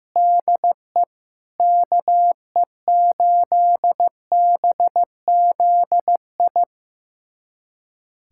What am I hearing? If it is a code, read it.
Morse code, heard as a keyed tone, about 15 wpm, 700 Hz. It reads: DE KE8BZI
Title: USA Callsign DE Callsign 15wpm